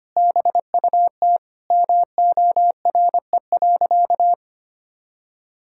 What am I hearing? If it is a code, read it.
Morse code, heard as a keyed tone, about 25 wpm, 700 Hz. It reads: BUT MORE.
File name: practice / 25wpm / Sentences with 2-Word Phrases - 25wpm